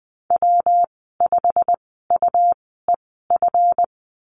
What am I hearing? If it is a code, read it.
Morse code, heard as a keyed tone, about 20 wpm, spaced as for 15 wpm, 700 Hz. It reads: W5UEF
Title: 147 Lesson 24g - Letter y - Call Signs 20wpm